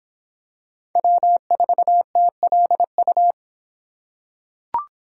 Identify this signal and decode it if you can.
Morse code, heard as a keyed tone, about 26 wpm, 700 Hz. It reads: W4TLU